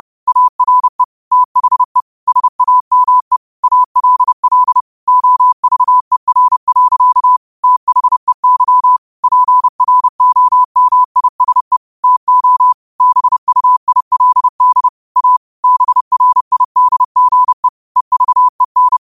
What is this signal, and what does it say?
Morse code, heard as a keyed tone, about 30 wpm, 1000 Hz. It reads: ARE THE SAME ALL OVER. THEY PROMISE TO BUILD A BRIDGE EVEN
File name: qotdfiles / 30wpm